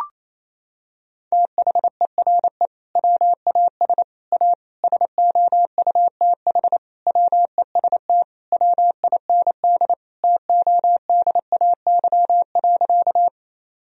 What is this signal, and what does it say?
Morse code, heard as a keyed tone, about 28 wpm, 700 Hz. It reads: THERE WAS A SOUTH WEST WIND TODAY.
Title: Sentences from Top 1000 Words - Review Part 2 - 28wpm